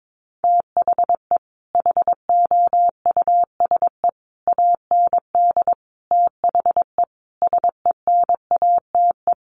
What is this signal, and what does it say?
Morse code, heard as a keyed tone, about 22 wpm, 700 Hz. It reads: THE HOUSE AND THE SENATE
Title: Sets of 5 Words Encore2 22wpm